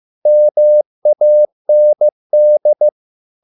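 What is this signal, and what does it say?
Morse code, heard as a keyed tone, about 15 wpm, 600 Hz. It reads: MAND
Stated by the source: Morse code at a slow 15 wpm from CW med Gnister.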